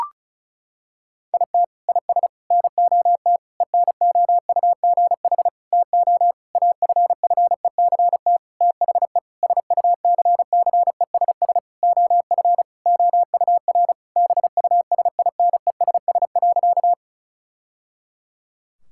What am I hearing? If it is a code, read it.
Morse code, heard as a keyed tone, about 34 wpm, 700 Hz. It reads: IT IS NOT ENOUGH TO AFFECT THE SUCCESS OF OUR BUSINESS.
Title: Sentences from Top 1000 Words - Review Part 6 - 35wpm